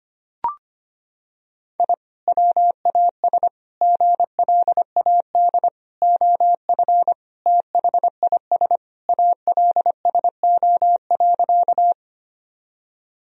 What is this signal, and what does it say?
Morse code, heard as a keyed tone, about 25 wpm, 700 Hz. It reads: I WAS GLAD OF THIS ALSO.
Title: Sentences from Top 1000 Words - Review Part 1 - 25wpm